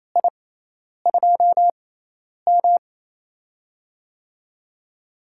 Morse code, heard as a keyed tone, about 28 wpm, spaced as for 10 wpm, 700 Hz. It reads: I2M
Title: USA Call Signs - 6x Character Spacing - 28wpm